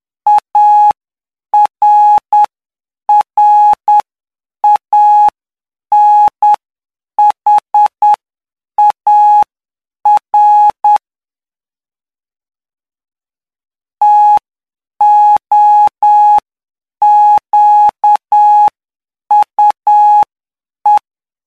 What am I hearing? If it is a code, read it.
Morse code, heard as a keyed tone, about 9 wpm, spaced as for 7 wpm, 840 Hz. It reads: ARRANHAR TOQUE
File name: cw / py2lcd / 12-PY2LCD-Li��o 21